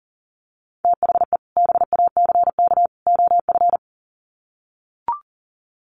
Morse code, heard as a keyed tone, about 38 wpm, 700 Hz. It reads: THE BACK OF
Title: Sets of 3 Words 40wpm